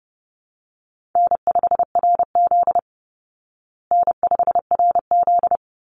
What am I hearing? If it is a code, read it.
Morse code, heard as a keyed tone, about 30 wpm, 700 Hz. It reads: N5RZ N5RZ
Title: Easy Call Signs Repeated 2x - 30wpm